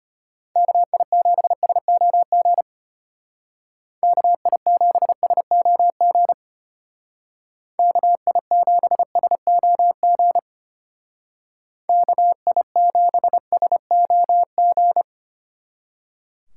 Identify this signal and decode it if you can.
Morse code, heard as a keyed tone, about 31 wpm, 700 Hz. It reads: KI7SOG KI7SOG KI7SOG KI7SOG
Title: Speed Racing - USA 2x3 Call Signs 25wpm